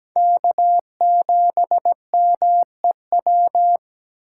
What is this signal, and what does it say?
Morse code, heard as a keyed tone, about 17 wpm, 700 Hz. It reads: K7MEW
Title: USA Call Signs - 5 Characters - 8x Character Spacing - 17wpm